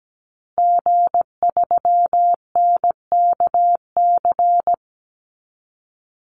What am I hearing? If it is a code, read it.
Morse code, heard as a keyed tone, about 17 wpm, 700 Hz. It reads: G3NKC
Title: Call Signs Repeated 2x - 17wpm